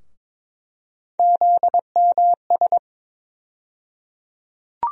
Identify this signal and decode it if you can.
Morse code, heard as a keyed tone, about 22 wpm, 700 Hz. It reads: ZMS
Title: USA Call Sign Postfixes - 3 Letters 22wpm